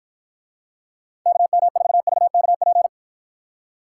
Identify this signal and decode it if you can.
Morse code, heard as a keyed tone, about 52 wpm, 700 Hz. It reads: KM4VKP